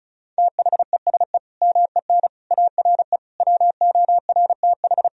Morse code, heard as a keyed tone, about 36 wpm, 700 Hz. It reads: THESE MEN ARE WORTH
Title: Sentences from Top 1000 Words - Review Part 1 - 35wpm